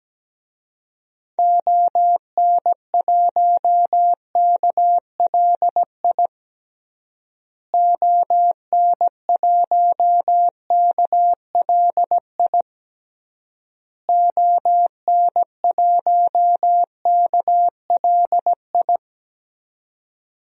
Morse code, heard as a keyed tone, about 17 wpm, 700 Hz. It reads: ON1KLI ON1KLI ON1KLI
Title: DXCC-100 Call Signs - 6 Characters - Repeated 3x - 17wpm